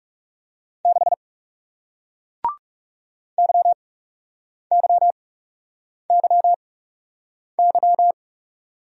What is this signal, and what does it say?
Morse code, heard as a keyed tone, about 40 wpm, spaced as for 8 wpm, 700 Hz. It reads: 6 YYYY